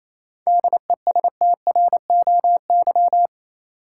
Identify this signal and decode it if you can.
Morse code, heard as a keyed tone, about 28 wpm, 700 Hz. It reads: DESTROY
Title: Seven Letter Words - 28wpm